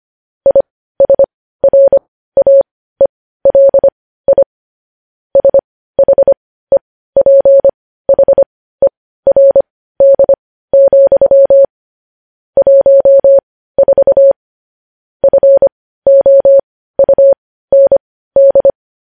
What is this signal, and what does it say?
Morse code, heard as a keyed tone, about 25 wpm, spaced as for 15 wpm, 550 Hz. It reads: ISRAELI SHEPHERD, 14 FOUND
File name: podcast / News headlines-1x-25x15